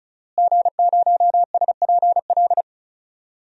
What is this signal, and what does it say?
Morse code, heard as a keyed tone, about 34 wpm, 700 Hz. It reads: G0SPL
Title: DXCC-100 Call Signs - 5 Characters - Repeated 3x - 35wpm